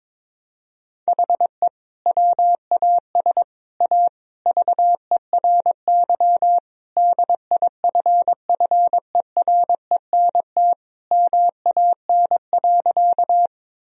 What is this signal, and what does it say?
Morse code, heard as a keyed tone, about 22 wpm, 700 Hz. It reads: HE WAS A VERY DIFFERENT MAN.
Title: Sentences from Top 1000 Words - Review Part 3 - 22wpm